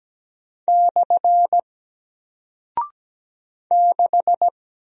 Morse code, heard as a keyed tone, about 17 wpm, 700 Hz. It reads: / 6
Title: Single Character (Letter:Number:Punctuation:Prosign) Alt 17wpm